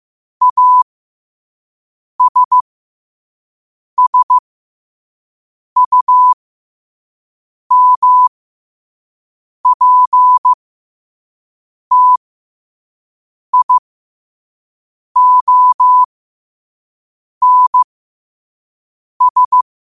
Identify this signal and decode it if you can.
Morse code, heard as a keyed tone, about 15 wpm, spaced as for 5 wpm, 1000 Hz. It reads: ASSUMPTIONS
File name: qotdfiles / 5wpm